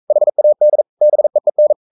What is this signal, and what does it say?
Morse code, heard as a keyed tone, about 42 wpm, 600 Hz. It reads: HAD BEEN
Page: Romanen Never af Ken Follet på engelsk.